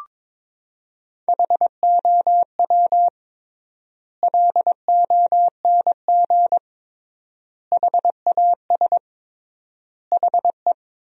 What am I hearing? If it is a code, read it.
Morse code, heard as a keyed tone, about 22 wpm, 700 Hz. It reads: HOW LONG HAS HE
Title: 3x Word-Spacing - Sentences from Top 100 Words - 22wpm